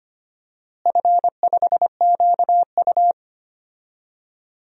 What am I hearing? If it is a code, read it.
Morse code, heard as a keyed tone, about 25 wpm, 700 Hz. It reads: F5QU
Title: DXCC-100 Call Signs - 4 Characters - Repeated 3x - 25wpm